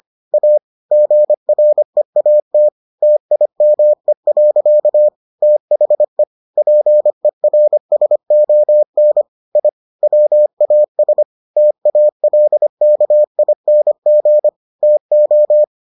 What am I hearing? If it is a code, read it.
Morse code, heard as a keyed tone, about 25 wpm, 600 Hz. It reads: A GREAT TIME. THE PERSON I WAS TALKING TO